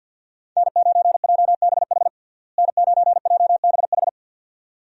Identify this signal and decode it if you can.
Morse code, heard as a keyed tone, about 52 wpm, 700 Hz. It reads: N9JBH N9JBH
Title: USA Call Signs - 5 Characters - Repeated 2x - 50wpm